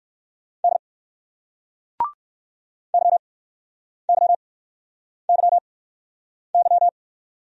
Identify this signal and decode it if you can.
Morse code, heard as a keyed tone, about 61 wpm, spaced as for 9 wpm, 700 Hz. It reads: D YYYY